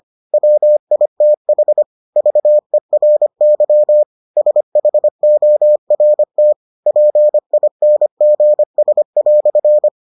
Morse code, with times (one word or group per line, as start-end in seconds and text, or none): 0.3-1.8 WITH
2.2-4.0 VERY
4.4-6.5 SHORT
6.9-9.9 PINGS"